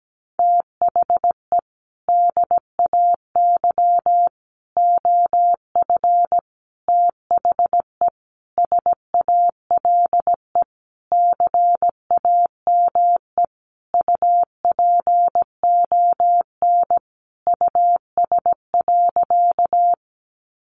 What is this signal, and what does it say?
Morse code, heard as a keyed tone, about 17 wpm, 700 Hz. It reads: THE DAY OF THE SALE CAME UPON US.
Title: Sentences from Top 1400 Words 17wpm